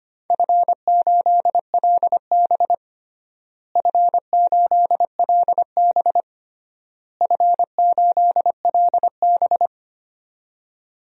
Morse code, heard as a keyed tone, about 25 wpm, 700 Hz. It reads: F8LB F8LB F8LB
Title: DXCC-100 Call Signs - 4 Characters - Repeated 3x - 25wpm